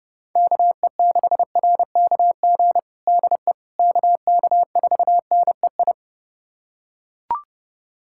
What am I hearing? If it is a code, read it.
Morse code, heard as a keyed tone, about 29 wpm, 700 Hz. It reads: KE6RKG DE KK4NEI
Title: USA Callsign DE Callsign 30wpm